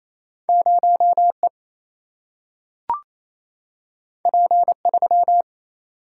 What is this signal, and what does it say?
Morse code, heard as a keyed tone, about 28 wpm, 700 Hz. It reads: 0 E P 3
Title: Single-Letter Single-Number 28wpm